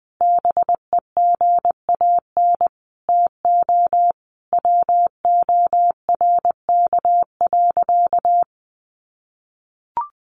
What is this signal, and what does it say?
Morse code, heard as a keyed tone, about 20 wpm, 700 Hz. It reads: BEGAN TO WORK.
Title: Sentences from Top 1100 Words 20wpm